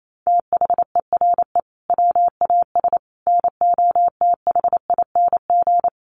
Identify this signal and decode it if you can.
Morse code, heard as a keyed tone, about 28 wpm, 700 Hz. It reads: THERE WAS NOTHING
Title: Sentences from Top 1100 Words 28wpm